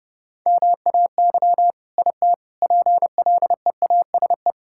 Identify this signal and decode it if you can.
Morse code, heard as a keyed tone, about 30 wpm, 700 Hz. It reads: MAY IT PLEASE
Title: Sentences from Top 1600 Words 30wpm